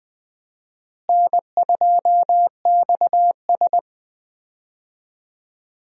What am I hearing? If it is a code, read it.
Morse code, heard as a keyed tone, about 20 wpm, 700 Hz. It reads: N2XS
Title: USA Call Signs - 4 Characters - Repeated 2x - 20wpm